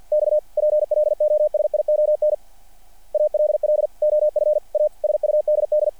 Morse code, heard as a keyed tone, about 50 wpm, 600 Hz. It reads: XPLOSION ALL OF A SUDD
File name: archived / CRSnet